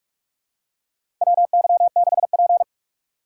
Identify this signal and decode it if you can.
Morse code, heard as a keyed tone, about 46 wpm, 700 Hz. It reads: WY6P
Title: USA Call Signs - 3x Character Spacing - 45wpm